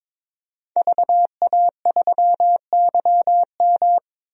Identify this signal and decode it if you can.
Morse code, heard as a keyed tone, about 22 wpm, 700 Hz. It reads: VA3YM
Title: DXCC-100 Call Signs - 5 Characters - 22wpm